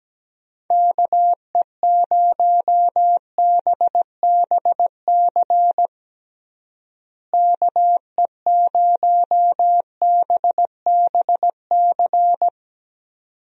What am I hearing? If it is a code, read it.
Morse code, heard as a keyed tone, about 17 wpm, 700 Hz. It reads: KE0BBC KE0BBC
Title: USA Call Signs - 6 Characters - Repeated 2x - 17wpm